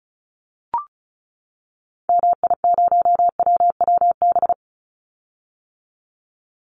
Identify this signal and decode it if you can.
Morse code, heard as a keyed tone, about 34 wpm, 700 Hz. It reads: MI0WWB
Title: Call Signs - 35wpm